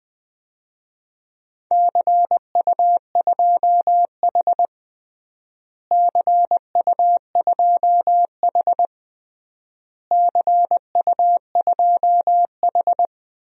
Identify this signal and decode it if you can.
Morse code, heard as a keyed tone, about 20 wpm, 700 Hz. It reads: CU2H CU2H CU2H
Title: DXCC-100 Call Signs - 4 Characters - Repeated 3x - 20wpm